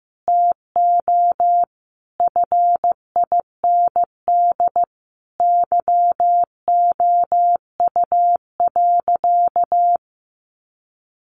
Morse code, heard as a keyed tone, about 14 wpm, 700 Hz. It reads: TO FIND YOU.
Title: Sentences from Top 1300 Words 15wpm